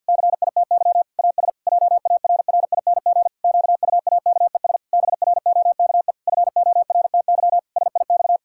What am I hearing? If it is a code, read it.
Morse code, heard as a keyed tone, about 50 wpm, 700 Hz. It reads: CITY AS JARRING QUAKES BROKE FORTY SIX
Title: Pangram Challenge v2 - 50wpm